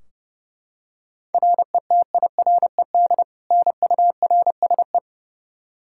Morse code, heard as a keyed tone, about 32 wpm, 700 Hz. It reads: RETIRED NURSE
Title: Practice QSO - Exchange 3 - More Info - 30wpm